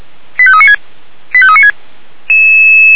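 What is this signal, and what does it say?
Five-tone selective call (ZVEI), heard as quick runs of five tones, tones of about 80 ms. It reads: 85396 85396
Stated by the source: Five-tone ZVEI selective call received by SDR.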